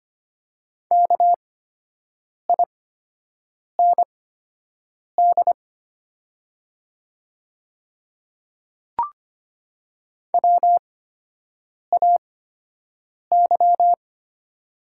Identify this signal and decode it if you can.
Morse code, heard as a keyed tone, about 25 wpm, spaced as for 7 wpm, 700 Hz. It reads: KIND WAY
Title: 8x Character Spacing - Top 200 Words - 25wpm